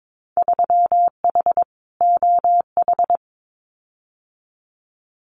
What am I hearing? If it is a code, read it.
Morse code, heard as a keyed tone, about 22 wpm, 700 Hz. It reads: 3H OH
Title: Winter Field Day Exchanges Encore 22wpm